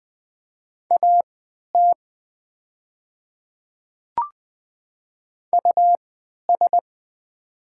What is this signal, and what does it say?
Morse code, heard as a keyed tone, about 20 wpm, spaced as for 11 wpm, 700 Hz. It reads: AT US
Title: 3x Character Spacing - Top 300 Words - 20wpm